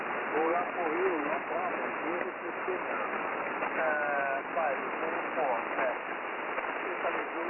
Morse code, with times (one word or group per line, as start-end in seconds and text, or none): none